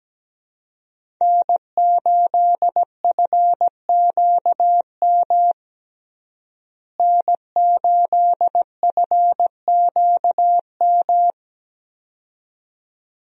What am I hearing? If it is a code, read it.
Morse code, heard as a keyed tone, about 17 wpm, 700 Hz. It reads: N8FQM N8FQM
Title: USA Call Signs - 5 Characters - Repeated 2x - 17wpm